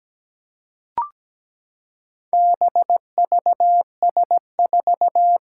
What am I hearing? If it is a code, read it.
Morse code, heard as a keyed tone, about 17 wpm, 700 Hz. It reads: BVS4
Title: Four Challenging Letter-Number Combinations 17wpm